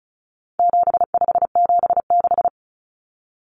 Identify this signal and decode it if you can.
Morse code, heard as a keyed tone, about 34 wpm, 700 Hz. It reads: 7576
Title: Common Cut Numbers - 4-Digit 35wpm